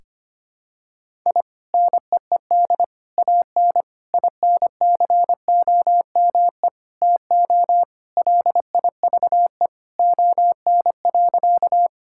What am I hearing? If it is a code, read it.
Morse code, heard as a keyed tone, about 25 wpm, 700 Hz. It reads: I NEED AN INCOME TO LIVE ON.
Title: Sentences from Top 1000 Words - Review Part 5 - 25wpm